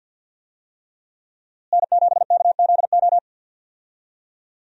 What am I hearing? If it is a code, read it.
Morse code, heard as a keyed tone, about 51 wpm, 700 Hz. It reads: N7KZO
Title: USA Call Signs - 5 Characters - Repeated 3x - 50wpm